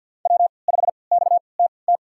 Morse code, heard as a keyed tone, about 51 wpm, spaced as for 29 wpm, 700 Hz. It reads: W5XTT
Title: USA Call Signs - 3x Character Spacing - 50wpm